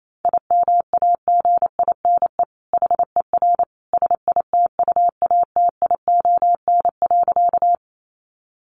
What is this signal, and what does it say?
Morse code, heard as a keyed tone, about 28 wpm, 700 Hz. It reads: IMAGINE HER SITUATION.
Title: Sentences from Top 1400 Words 28wpm